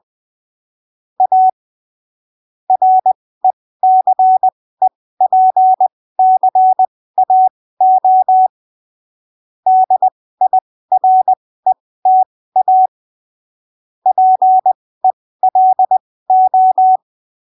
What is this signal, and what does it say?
Morse code, heard as a keyed tone, about 20 wpm, spaced as for 15 wpm, 750 Hz. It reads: A RECEPCAO DIRETA PELO
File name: arquivos / texto / texto 01 20ppm